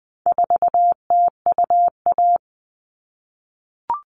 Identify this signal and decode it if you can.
Morse code, heard as a keyed tone, about 20 wpm, 700 Hz. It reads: 4TUA
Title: All Cut Numbers - 4-Digit 20wpm